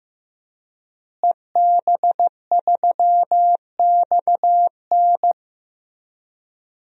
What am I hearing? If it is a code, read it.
Morse code, heard as a keyed tone, about 15 wpm, 700 Hz. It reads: EB3XN
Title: DXCC-100 Call Signs - 5 Characters - Repeated 3x - 15wpm